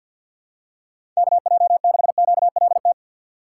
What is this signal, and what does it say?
Morse code, heard as a keyed tone, about 47 wpm, 700 Hz. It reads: KJ6QLT
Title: USA Call Signs - 6x Character Spacing - 50wpm